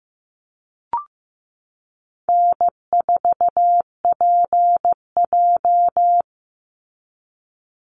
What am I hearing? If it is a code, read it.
Morse code, heard as a keyed tone, about 15 wpm, 700 Hz. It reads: N4PJ
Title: Easy Call Signs Repeated 2x - 15wpm